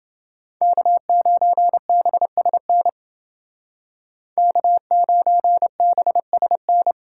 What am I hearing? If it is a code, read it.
Morse code, heard as a keyed tone, about 28 wpm, 700 Hz. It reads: K9BSN K9BSN